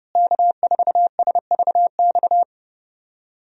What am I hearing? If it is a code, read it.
Morse code, heard as a keyed tone, about 30 wpm, 700 Hz. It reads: K4SVX
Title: USA Call Signs - 5 Characters - Repeated 3x - 30wpm